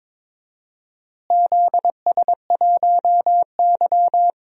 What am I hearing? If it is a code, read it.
Morse code, heard as a keyed tone, about 22 wpm, 700 Hz. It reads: ZS1Y
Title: DXCC-100 Call Signs - 4 Characters - Repeated 2x - 22wpm